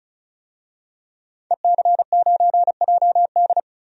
Morse code, heard as a keyed tone, about 36 wpm, 700 Hz. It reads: EC9JD
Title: DXCC-100 Call Signs - 5 Characters - 35wpm